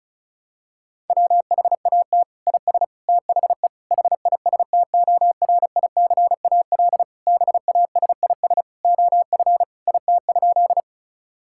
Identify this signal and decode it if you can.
Morse code, heard as a keyed tone, about 34 wpm, 700 Hz. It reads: WHAT IS THE HISTORICAL BASIS OF IT?
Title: Sentences from Top 1000 Words - Review Part 5 - 35wpm